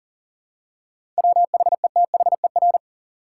Morse code, heard as a keyed tone, about 40 wpm, 700 Hz. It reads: WHETHER
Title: Seven Letter Words - 40wpm